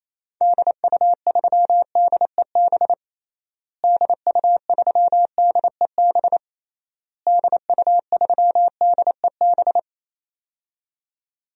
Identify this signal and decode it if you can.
Morse code, heard as a keyed tone, about 28 wpm, 700 Hz. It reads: DU3DEB DU3DEB DU3DEB
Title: DXCC-100 Call Signs - 6 Characters - Repeated 3x - 28wpm